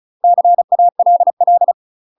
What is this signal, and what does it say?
Morse code, heard as a keyed tone, about 36 wpm, 700 Hz. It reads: CALL
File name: UK MASTER QSOs TRAINING / 35WPM / UK MASTER QSOs 0023